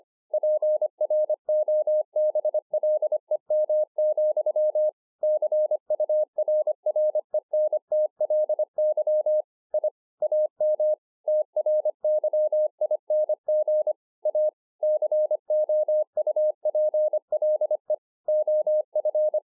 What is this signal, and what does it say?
Morse code, heard as a keyed tone, about 25 wpm, 610 Hz. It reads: PROBLEM, CURRENTLY I AM TRYING A COUPLE OF
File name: 2011-04-mp3 / THIS IS NOT MY SHACK 2 25WPM